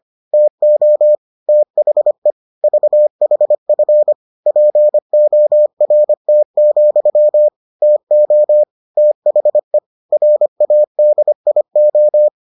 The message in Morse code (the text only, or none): TO THE VHF PORT, TO THE RADIO